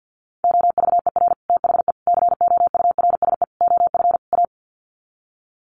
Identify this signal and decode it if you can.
Morse code, heard as a keyed tone, about 49 wpm, 700 Hz. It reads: OVER THE COURSE OF A
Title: Sets of 5 Words Encore 50wpm